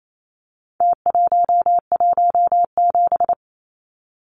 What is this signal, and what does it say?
Morse code, heard as a keyed tone, about 28 wpm, 700 Hz. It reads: T117
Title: Numbers - Contest Style - 4-Digit 28wpm